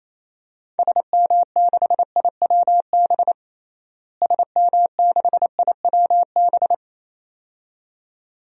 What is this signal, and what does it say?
Morse code, heard as a keyed tone, about 28 wpm, 700 Hz. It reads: SM6IWB SM6IWB
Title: DXCC-100 Call Signs - 6 Characters - Repeated 2x - 28wpm